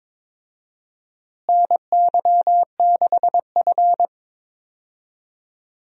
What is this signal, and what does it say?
Morse code, heard as a keyed tone, about 22 wpm, 700 Hz. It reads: NY6F
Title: USA Call Signs - 4 Characters - Repeated 3x - 22wpm